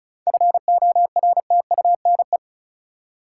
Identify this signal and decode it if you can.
Morse code, heard as a keyed tone, about 36 wpm, 700 Hz. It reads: FORTUNE
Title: Seven Letter Words - 35wpm